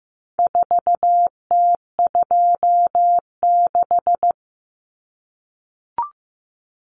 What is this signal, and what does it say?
Morse code, heard as a keyed tone, about 15 wpm, 700 Hz. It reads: 4T26
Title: Common Cut Numbers - 4-Digit 15wpm